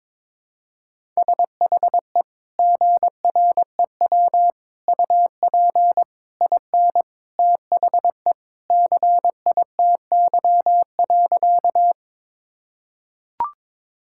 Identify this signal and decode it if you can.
Morse code, heard as a keyed tone, about 22 wpm, 700 Hz. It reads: SHE GREW UP IN THE CITY.
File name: practice / 22wpm / Sentences with 2-Word Phrases - 22wpm